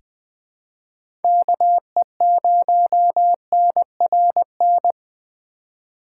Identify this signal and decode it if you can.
Morse code, heard as a keyed tone, about 20 wpm, 700 Hz. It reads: KE0NRN
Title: USA Call Signs - 4x Character Spacing - 20wpm